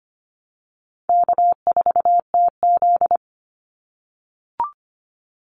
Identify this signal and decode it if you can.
Morse code, heard as a keyed tone, about 25 wpm, 700 Hz. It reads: K4TZ
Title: Call Signs - 25wpm